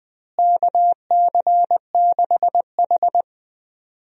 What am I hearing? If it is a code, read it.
Morse code, heard as a keyed tone, about 20 wpm, 700 Hz. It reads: KC6H
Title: USA Call Signs - 4 Characters - Repeated 2x - 20wpm